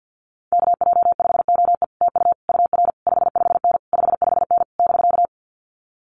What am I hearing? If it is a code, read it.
Morse code, heard as a keyed tone, about 50 wpm, 700 Hz. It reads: KJ5OE TU UR 55N 55N <BK>
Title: Common QSO Phrases Encore 50wpm